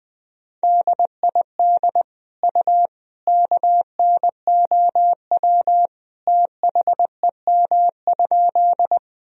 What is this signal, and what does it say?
Morse code, heard as a keyed tone, about 20 wpm, 700 Hz. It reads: DID U KNOW THEM?
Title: Sentences from Top 100 Words and Abbreviations 20wpm